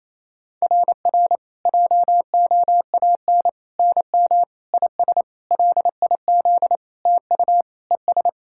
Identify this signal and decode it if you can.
Morse code, heard as a keyed tone, about 28 wpm, 690 Hz. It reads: RR JOAN NM IS LIZ TU ES
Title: SOTA QSO Practice 28wpm